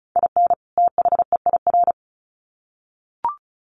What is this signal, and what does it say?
Morse code, heard as a keyed tone, about 34 wpm, 700 Hz. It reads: IN THEIR
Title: Sets of 2 Words 35wpm